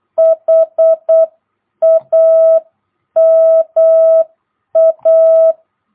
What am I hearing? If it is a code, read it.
Morse code, heard as a keyed tone, about 8 wpm, 660 Hz. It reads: HAMA
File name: jh8cbh / 2022cw